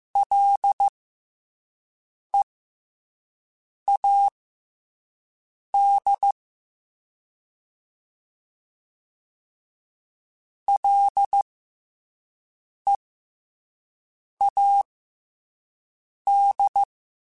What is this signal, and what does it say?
Morse code, heard as a keyed tone, about 15 wpm, spaced as for 5 wpm, 800 Hz. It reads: LEAD LEAD
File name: F and L / F and L mp3 / F and L Prefixes 3x2 Twice @5